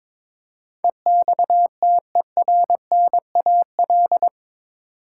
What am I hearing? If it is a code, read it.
Morse code, heard as a keyed tone, about 22 wpm, 700 Hz. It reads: EXTERNAL
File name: practice / 22wpm / Eight Letter Words - 22wpm